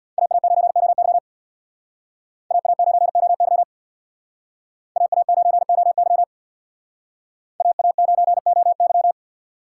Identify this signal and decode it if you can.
Morse code, heard as a keyed tone, about 62 wpm, 700 Hz. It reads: AA9OY AA9OY AA9OY AA9OY
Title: Speed Racing - USA 2x2 Call Signs 50wpm